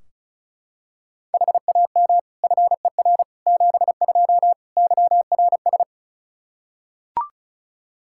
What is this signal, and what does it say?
Morse code, heard as a keyed tone, about 34 wpm, 700 Hz. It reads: HAM FER 72 YRS
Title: Practice QSO - Exchange 3 - More Info - 35wpm